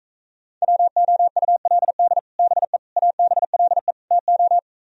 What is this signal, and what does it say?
Morse code, heard as a keyed tone, about 38 wpm, 700 Hz. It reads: WOULD BE ABLE TO